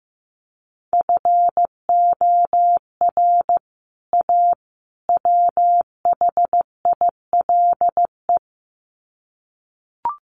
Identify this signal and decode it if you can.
Morse code, heard as a keyed tone, about 15 wpm, 700 Hz. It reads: FOR A WHILE
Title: Sets of 3 Words 15wpm